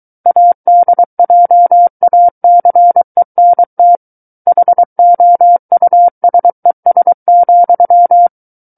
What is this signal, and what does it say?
Morse code, heard as a keyed tone, about 23 wpm, 700 Hz. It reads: ADJACENT HOUSES,
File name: Book One / 14-Chapter 14 - 23 WPM